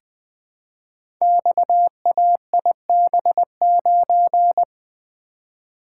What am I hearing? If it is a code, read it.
Morse code, heard as a keyed tone, about 20 wpm, 700 Hz. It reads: XAIB9
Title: Code Groups - 5 Letter:Numbers - 20wpm